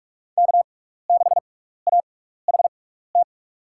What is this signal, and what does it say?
Morse code, heard as a keyed tone, about 46 wpm, spaced as for 16 wpm, 700 Hz. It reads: K6AHT
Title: USA Call Signs - 5 Characters - 6x Character Spacing - 45wpm